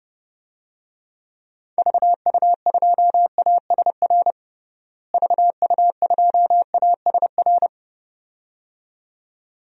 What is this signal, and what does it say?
Morse code, heard as a keyed tone, about 30 wpm, 700 Hz. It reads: VU2ASR VU2ASR
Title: DXCC-100 Call Signs - 6 Characters - Repeated 2x - 30wpm